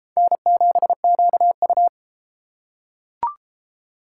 Morse code, heard as a keyed tone, about 33 wpm, 700 Hz. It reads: N7QU